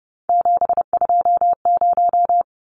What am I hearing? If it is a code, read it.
Morse code, heard as a keyed tone, about 30 wpm, 700 Hz. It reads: 720
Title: Numbers - 3-Digit 30wpm